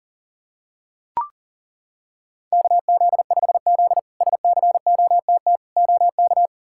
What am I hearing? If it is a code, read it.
Morse code, heard as a keyed tone, about 40 wpm, 700 Hz. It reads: KZ5Z SCOTT OK
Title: K1USN SST QSO Practice - 40wpm